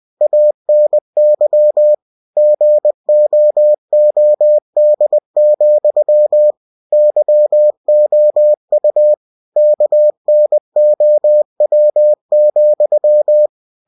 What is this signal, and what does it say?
Morse code, heard as a keyed tone, about 20 wpm, 600 Hz. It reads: ANY GOOD, YOU KNOW,
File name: cw-books / winnie-the-pooh / winnie-20wpm-page-0084